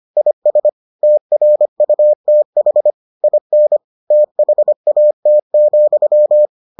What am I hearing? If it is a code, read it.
Morse code, heard as a keyed tone, about 25 wpm, 600 Hz. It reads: IS TRUTH IN THAT,